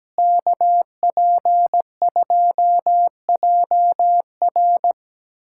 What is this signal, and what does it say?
Morse code, heard as a keyed tone, about 17 wpm, 700 Hz. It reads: KP2JR
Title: USA Callsign DE Callsign - 5 Characters - 17wpm